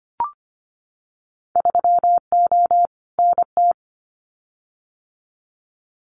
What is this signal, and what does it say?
Morse code, heard as a keyed tone, about 25 wpm, 700 Hz. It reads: 3O NT
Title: Winter Field Day Exchanges Encore 25wpm